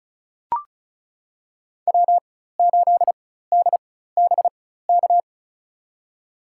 Morse code, heard as a keyed tone, about 34 wpm, spaced as for 16 wpm, 700 Hz. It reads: W8DBK
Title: USA Call Signs - 5 Characters - 4x Character Spacing - 35wpm